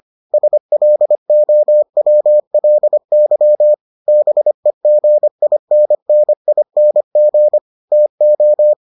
Morse code, heard as a keyed tone, about 25 wpm, 600 Hz. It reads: SLOWLY BEGINNING TO